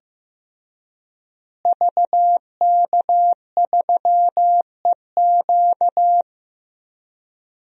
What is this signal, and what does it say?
Morse code, heard as a keyed tone, about 15 wpm, 700 Hz. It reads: VK3EQ
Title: DXCC-100 Call Signs - 5 Characters - Repeated 3x - 15wpm